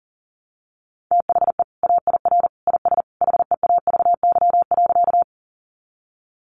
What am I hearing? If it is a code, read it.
Morse code, heard as a keyed tone, about 40 wpm, 700 Hz. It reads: THE AIR IS HEAVY.
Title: Sentences from Top 1100 Words 40wpm